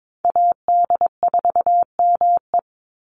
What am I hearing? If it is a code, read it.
Morse code, heard as a keyed tone, about 22 wpm, 700 Hz. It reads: AD4ME
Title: USA Call Signs 22wpm